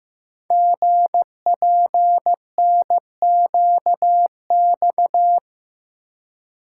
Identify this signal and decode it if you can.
Morse code, heard as a keyed tone, about 15 wpm, 700 Hz. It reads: GPNQX
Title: Code Groups - 5 Letter:Numbers - 15wpm